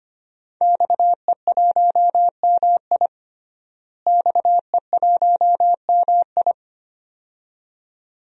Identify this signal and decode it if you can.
Morse code, heard as a keyed tone, about 25 wpm, 700 Hz. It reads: XE1MI XE1MI
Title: DXCC-100 Call Signs - 5 Characters - Repeated 2x - 25wpm